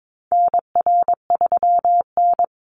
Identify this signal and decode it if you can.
Morse code, heard as a keyed tone, about 22 wpm, 700 Hz. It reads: NR3N
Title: Call Signs Repeated 3x - 22wpm